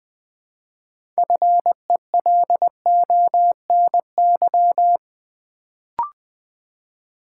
Morse code, heard as a keyed tone, about 20 wpm, 700 Hz. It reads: FELONY
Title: Build A Word - 3 Letters At A Time - 6-Letter Words - 20wpm